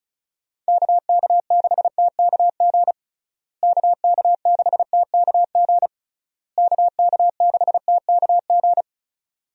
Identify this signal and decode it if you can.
Morse code, heard as a keyed tone, about 34 wpm, 700 Hz. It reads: KK6TKG KK6TKG KK6TKG
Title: USA Call Signs - 6 Characters - Repeated 3x - 35wpm